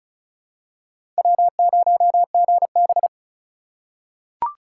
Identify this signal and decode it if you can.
Morse code, heard as a keyed tone, about 36 wpm, 700 Hz. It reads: W0GB
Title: USA Call Signs - 4 Characters - 2x Character Spacing - 35wpm